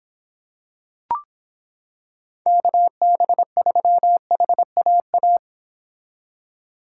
Morse code, heard as a keyed tone, about 26 wpm, 700 Hz. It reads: KB3HAA